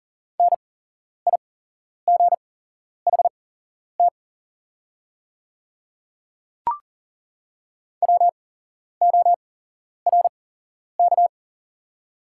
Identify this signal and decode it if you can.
Morse code, heard as a keyed tone, about 38 wpm, spaced as for 11 wpm, 700 Hz. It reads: NIGHT WORK
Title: 8x Character Spacing - Top 300 Words - 40wpm